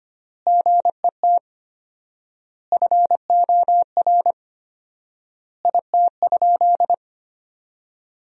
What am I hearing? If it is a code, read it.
Morse code, heard as a keyed tone, about 25 wpm, 700 Hz. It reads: GET FOR IT?
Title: 4x Word-Spacing - Sentences from Top 400 Words - 25wpm